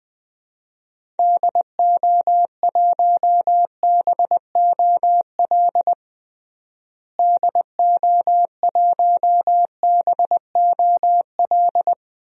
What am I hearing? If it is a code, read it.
Morse code, heard as a keyed tone, about 20 wpm, 700 Hz. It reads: DO1BOL DO1BOL
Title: DXCC-100 Call Signs - 6 Characters - Repeated 2x - 20wpm